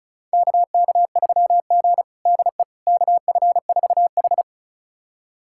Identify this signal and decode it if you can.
Morse code, heard as a keyed tone, about 34 wpm, 700 Hz. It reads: KK3G DE KF4H
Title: USA Callsign DE Callsign - 4 Characters - 35wpm